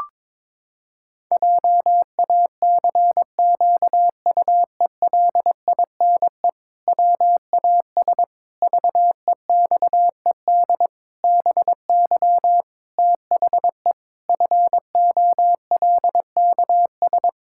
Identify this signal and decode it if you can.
Morse code, heard as a keyed tone, about 22 wpm, 700 Hz. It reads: JACQUELINE WAS VEXED BY THE FOLKS
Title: Pangram Challenge v2 - 22wpm